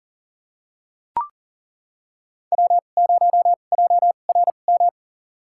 Morse code, heard as a keyed tone, about 40 wpm, spaced as for 29 wpm, 700 Hz. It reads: W0JRM
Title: USA Call Signs - 5 Characters - 2x Character Spacing - 40wpm